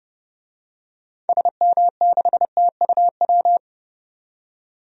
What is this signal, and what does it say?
Morse code, heard as a keyed tone, about 32 wpm, 700 Hz. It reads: SM6TUW
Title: DXCC-100 Call Signs - 6 Characters - 30wpm